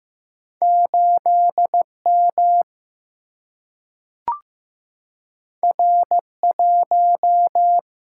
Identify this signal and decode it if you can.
Morse code, heard as a keyed tone, about 15 wpm, 700 Hz. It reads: 8 M R 1
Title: Single-Letter Single-Number 15wpm